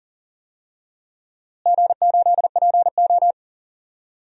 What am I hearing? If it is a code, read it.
Morse code, heard as a keyed tone, about 40 wpm, 700 Hz. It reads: G8PO
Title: DXCC-100 Call Signs - 4 Characters - Repeated 3x - 40wpm